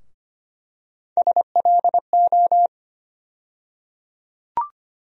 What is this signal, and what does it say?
Morse code, heard as a keyed tone, about 25 wpm, 700 Hz. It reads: SLO
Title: USA Call Sign Postfixes - 3 Letters 25wpm